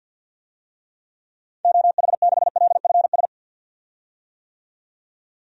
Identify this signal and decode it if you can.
Morse code, heard as a keyed tone, about 52 wpm, 700 Hz. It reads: OH6LFS
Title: DXCC-100 Call Signs - 6 Characters - 50wpm